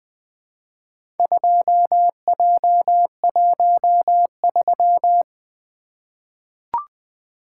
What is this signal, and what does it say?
Morse code, heard as a keyed tone, about 20 wpm, 700 Hz. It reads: 2J13
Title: Four Letter-Number Combinations - 20wpm